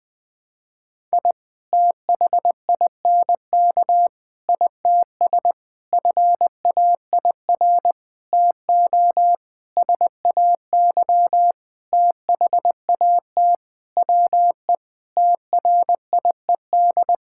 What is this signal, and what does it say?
Morse code, heard as a keyed tone, about 20 wpm, 700 Hz. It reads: I THINK ITS FAIR TO SAY THAT WE TRIED
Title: Sentences with 7-Word Phrases - Spoken Beginning - 20wpm